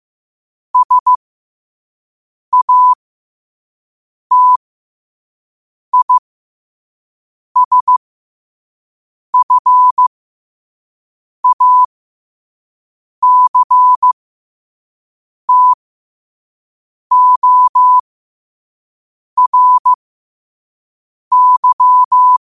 Morse code, heard as a keyed tone, about 15 wpm, spaced as for 5 wpm, 1000 Hz. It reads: SATISFACTORY
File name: qotdfiles / 5wpm